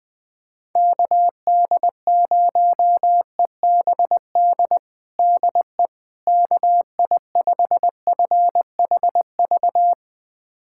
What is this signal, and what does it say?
Morse code, heard as a keyed tone, about 21 wpm, 700 Hz. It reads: KD0EBD DE KI5FHV
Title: USA Callsign DE Callsign - 6 Characters - 20wpm